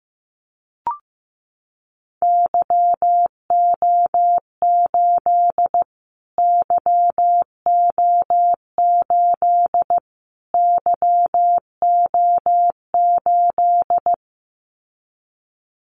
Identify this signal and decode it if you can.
Morse code, heard as a keyed tone, about 15 wpm, 700 Hz. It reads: YO8 YO8 YO8
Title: Callsign Prefixes for Top 25 DXCC - Repeated 3x - 15wpm